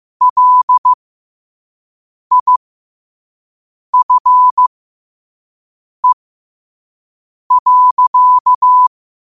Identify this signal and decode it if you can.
Morse code, heard as a keyed tone, about 15 wpm, spaced as for 5 wpm, 1000 Hz. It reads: LIFE.
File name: qotdfiles / 5wpm